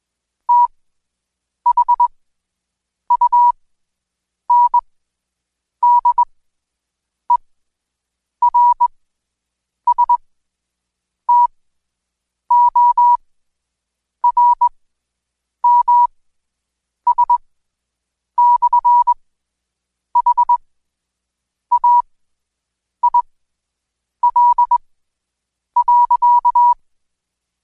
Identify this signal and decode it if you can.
Morse code, heard as a keyed tone, about 21 wpm, spaced as for 7 wpm, 970 Hz. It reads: THUNDERSTORMS/HAIL.
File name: Code Quick / AudioDownld2 / Seg112-Day17-DayDrill 17